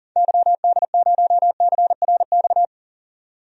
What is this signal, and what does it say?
Morse code, heard as a keyed tone, about 40 wpm, 700 Hz. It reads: YD0CRX
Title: DXCC-100 Call Signs - 6 Characters - 40wpm